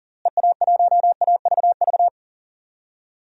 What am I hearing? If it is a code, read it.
Morse code, heard as a keyed tone, about 39 wpm, 700 Hz. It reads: EA1AVV
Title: DXCC-100 Call Signs - 6 Characters - Repeated 3x - 40wpm